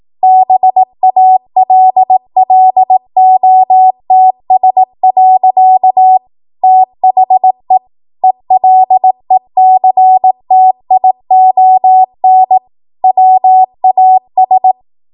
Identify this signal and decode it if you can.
Morse code, heard as a keyed tone, about 18 wpm, 750 Hz. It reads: BALLOTS. THE ELECTION WAS